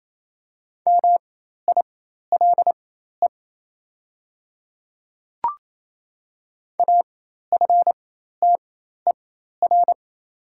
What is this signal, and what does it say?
Morse code, heard as a keyed tone, about 27 wpm, spaced as for 13 wpm, 700 Hz. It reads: MILE AFTER
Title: 4x Character Spacing - Top 500 Words - 28wpm